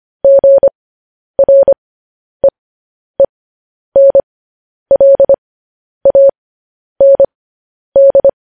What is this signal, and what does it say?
Morse code, heard as a keyed tone, about 25 wpm, spaced as for 10 wpm, 550 Hz. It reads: GREENLAND
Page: news-headlines-in-morse-code-at-10-wpm